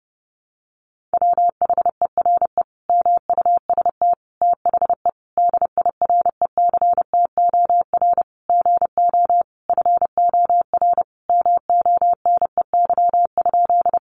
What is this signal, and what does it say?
Morse code, heard as a keyed tone, about 30 wpm, 700 Hz. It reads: WHERE MUST THE DIRECTOR GO FOR MONEY?
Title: Sentences from Top 1300 Words 30wpm